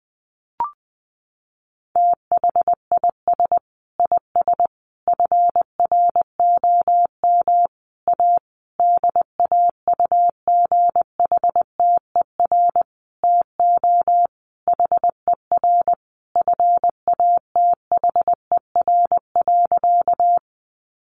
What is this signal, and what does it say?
Morse code, heard as a keyed tone, about 20 wpm, 700 Hz. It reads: THIS IS FROM A DAUGHTER TO HER FATHER.
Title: Sentences from Top 1500 Words 20wpm